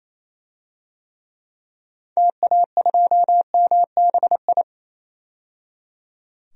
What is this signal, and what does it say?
Morse code, heard as a keyed tone, about 28 wpm, 700 Hz. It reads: TA2MBI
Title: DXCC-100 Call Signs - 6 Characters - 28wpm